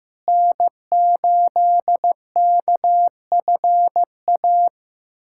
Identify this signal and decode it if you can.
Morse code, heard as a keyed tone, about 15 wpm, 700 Hz. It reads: N8KFA
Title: USA Callsign DE Callsign - 5 Characters - 15wpm